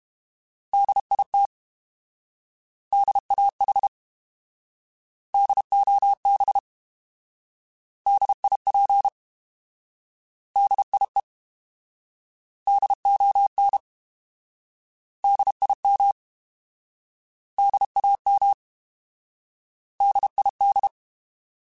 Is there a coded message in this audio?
Yes: Morse code from 0.7 to 20.9 s, heard as a keyed tone, about 32 wpm, 800 Hz. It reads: DIT DAH DOB DIP DIE DON DIM DAM DID